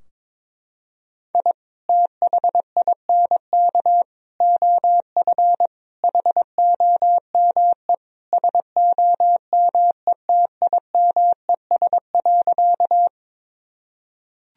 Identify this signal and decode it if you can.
Morse code, heard as a keyed tone, about 22 wpm, 700 Hz. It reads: I THINK OF HOME SOMETIMES.
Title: Sentences from Top 1000 Words - Review Part 3 - 22wpm